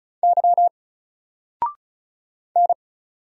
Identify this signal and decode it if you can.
Morse code, heard as a keyed tone, about 34 wpm, 700 Hz. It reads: Y N
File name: Single-Letter-Number-35wpm